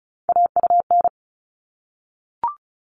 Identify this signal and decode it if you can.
Morse code, heard as a keyed tone, about 34 wpm, 700 Hz. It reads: AUN
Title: All Cut Numbers - 3-Digit 35wpm